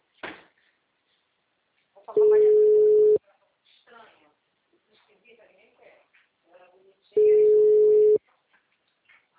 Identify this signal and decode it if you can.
Ringback tone (Europe, 425 Hz), with speech.
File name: out- -2003- - - .200